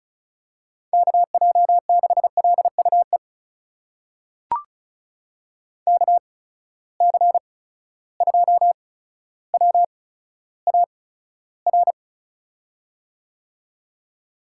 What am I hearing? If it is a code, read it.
Morse code, heard as a keyed tone, about 36 wpm, spaced as for 15 wpm, 700 Hz. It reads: KJ6LUE KC2WAR
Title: USA Call Signs - 6 Characters - 8x Character Spacing - 35wpm